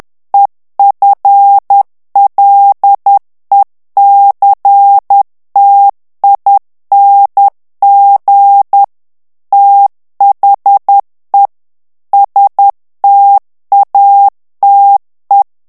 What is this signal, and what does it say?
Morse code, heard as a keyed tone, about 10 wpm, 800 Hz. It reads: EFLECTING THE STATE
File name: ANGLAIS texte clair 3 vitesse 10 mots minute